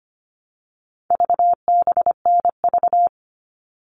Morse code, heard as a keyed tone, about 25 wpm, 700 Hz. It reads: VBNV
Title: All Cut Numbers - 4-Digit 25wpm